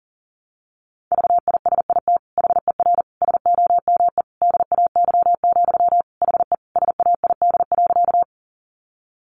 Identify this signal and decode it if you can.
Morse code, heard as a keyed tone, about 40 wpm, 700 Hz. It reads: VISIT HER SOME DAY, HE SAID.
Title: Sentences from Top 900 Words 40wpm